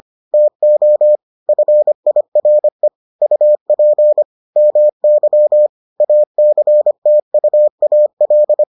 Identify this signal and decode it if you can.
Morse code, heard as a keyed tone, about 25 wpm, 600 Hz. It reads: TO FIRE UP MY ACTUAL